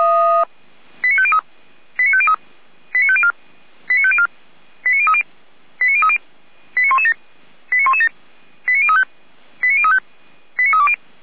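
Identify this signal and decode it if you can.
Five-tone selective call (ZVEI), heard as quick runs of five tones, tones of about 70 ms. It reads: 89503 89503 89504 89504 89030 89030 89117 89117 89035 89035 89320
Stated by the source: ZVEI five-tone selective call 89042, received by SDR.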